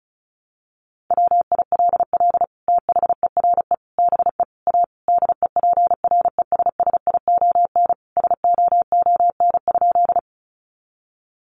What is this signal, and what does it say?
Morse code, heard as a keyed tone, about 34 wpm, 700 Hz. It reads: WILL THERE BE A DEPRESSION SOON?
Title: Sentences from Top 1300 Words 35wpm